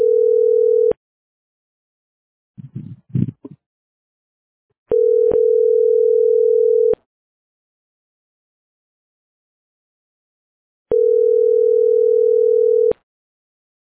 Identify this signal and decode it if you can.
Ringback tone (North America), 440 + 480 Hz.